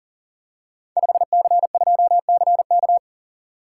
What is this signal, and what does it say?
Morse code, heard as a keyed tone, about 43 wpm, 690 Hz. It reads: 5C2CK
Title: Code Groups - 5 Letter:Numbers - 40wpm